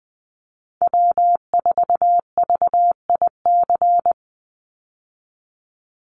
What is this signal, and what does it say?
Morse code, heard as a keyed tone, about 20 wpm, 700 Hz. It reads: W4VIC
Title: Call Signs Repeated 2x - 20wpm